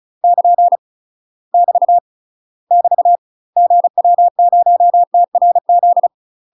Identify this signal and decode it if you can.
Morse code, heard as a keyed tone, about 36 wpm, 700 Hz. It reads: <KN> <BT> <BT> GW0TRZ
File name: UK MASTER QSOs TRAINING / 35WPM / UK MASTER QSOs 0026